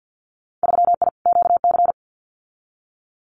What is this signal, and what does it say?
Morse code, heard as a keyed tone, about 50 wpm, 700 Hz. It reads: 3I QC
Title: Winter Field Day Exchanges Encore 50wpm